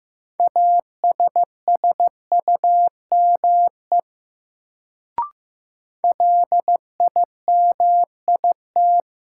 Morse code, heard as a keyed tone, about 15 wpm, 700 Hz. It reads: ASSUME LIMIT
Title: Sentences from Top 1500 Words - New Word Drill - 15wpm